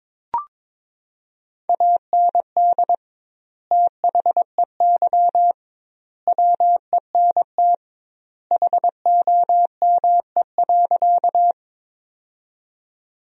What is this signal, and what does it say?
Morse code, heard as a keyed tone, about 22 wpm, 700 Hz. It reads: AND THEY WENT HOME.
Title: 2x Word-Spacing - Sentences from Top 200 Words - 22wpm